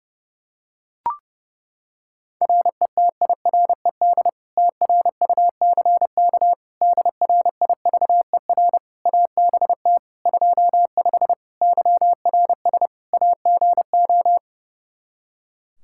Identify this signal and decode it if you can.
Morse code, heard as a keyed tone, about 30 wpm, 700 Hz. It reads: RETIRED TRUCK DRIVER ABT 25 YRS AGO
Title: Practice QSO - Exchange 3 - More Info - 30wpm